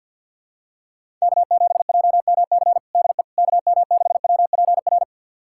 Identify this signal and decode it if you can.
Morse code, heard as a keyed tone, about 50 wpm, 700 Hz. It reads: K7JMC DE KM6WPR
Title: USA Callsign DE Callsign 50wpm